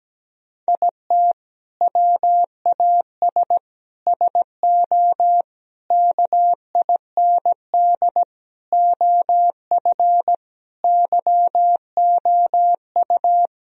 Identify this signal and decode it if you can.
Morse code, heard as a keyed tone, about 17 wpm, 700 Hz. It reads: IT WAS SO KIND OF YOU
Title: Sentences from Top 1000 Words - Review Part 3 - 17wpm